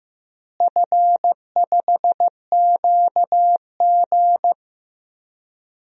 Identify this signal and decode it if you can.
Morse code, heard as a keyed tone, about 15 wpm, 700 Hz. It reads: F5QG
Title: DXCC-100 Call Signs - 4 Characters - Repeated 2x - 15wpm